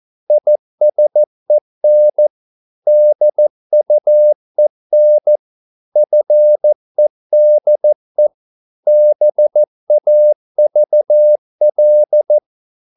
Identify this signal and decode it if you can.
Morse code, heard as a keyed tone, about 14 wpm, 600 Hz. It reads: ISEN DUEN FEDE BAVL
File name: Korte ord DK 14wpm